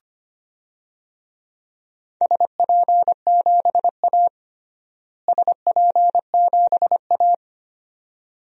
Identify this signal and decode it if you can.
Morse code, heard as a keyed tone, about 25 wpm, 700 Hz. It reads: SP7A SP7A
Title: DXCC-100 Call Signs - 4 Characters - Repeated 2x - 25wpm